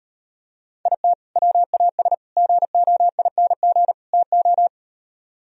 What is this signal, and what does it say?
Morse code, heard as a keyed tone, about 38 wpm, 700 Hz. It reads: IT WAS GOING TO